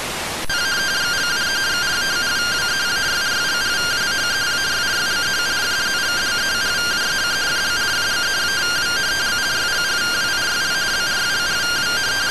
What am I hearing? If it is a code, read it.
Two-tone teleprinter signal, about 170 Hz shift.
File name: findings / 202 coronal iodine 3mm.gen-11 rtty rx